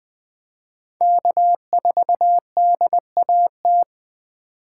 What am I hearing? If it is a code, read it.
Morse code, heard as a keyed tone, about 20 wpm, 700 Hz. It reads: K4DAT
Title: USA Call Signs - 5 Characters - 4x Character Spacing - 20wpm